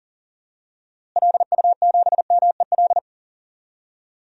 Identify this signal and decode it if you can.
Morse code, heard as a keyed tone, about 40 wpm, 700 Hz. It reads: LU7MEL
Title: DXCC-100 Call Signs - 6 Characters - Repeated 3x - 40wpm